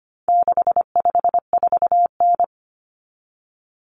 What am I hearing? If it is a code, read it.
Morse code, heard as a keyed tone, about 25 wpm, 700 Hz. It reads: 654N
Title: Common Cut Numbers - 4-Digit 25wpm